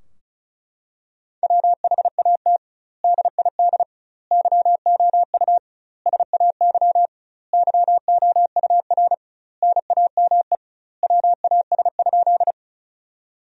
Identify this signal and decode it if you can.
Morse code, heard as a keyed tone, about 36 wpm, 700 Hz. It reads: WHAT DID YOU SAY YOUR NAME WAS?
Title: 2x Word-Spacing - Sentences from Top 200 Words - 35wpm